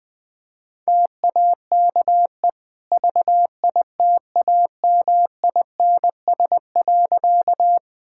Morse code, heard as a keyed tone, about 20 wpm, 700 Hz. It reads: TAKE VITAMINS.
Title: Easy Sentences with Hints - 20wpm